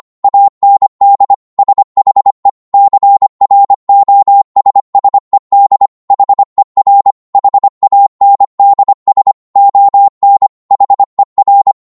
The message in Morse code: AND SHE CROSSED HER HANDS ON HER